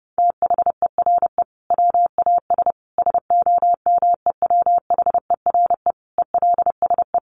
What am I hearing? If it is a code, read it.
Morse code, heard as a keyed tone, about 30 wpm, 700 Hz. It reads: THERE WAS SOMEWHERE ELSE
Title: Sentences from Top 1400 Words 30wpm